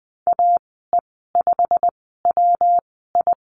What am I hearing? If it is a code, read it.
Morse code, heard as a keyed tone, about 20 wpm, spaced as for 14 wpm, 700 Hz. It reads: AE5WI
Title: 213 Lesson 34g - Number 8 - Call Signs 20wpm